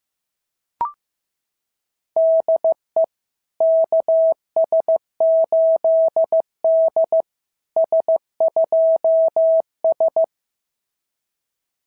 Morse code, heard as a keyed tone, about 15 wpm, 650 Hz. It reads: DE KS8D S2S
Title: SOTA QSO Practice 15wpm